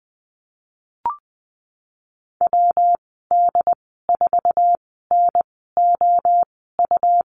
Morse code, heard as a keyed tone, about 20 wpm, spaced as for 15 wpm, 700 Hz. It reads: WD4NOU
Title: 122 Lesson 20g - Number 3 - Call Signs 20wpm